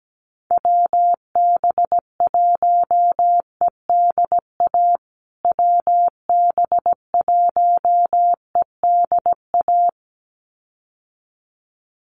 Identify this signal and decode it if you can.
Morse code, heard as a keyed tone, about 17 wpm, 700 Hz. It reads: WB1EDA WB1EDA
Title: USA Call Signs Repeated 2x 17wpm